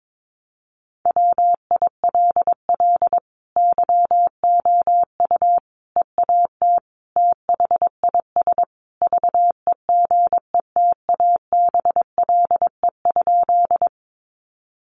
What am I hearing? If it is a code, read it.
Morse code, heard as a keyed tone, about 22 wpm, 700 Hz. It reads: WILL YOU EAT THIS VEGETABLE?
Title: Sentences from Top 1300 Words 22wpm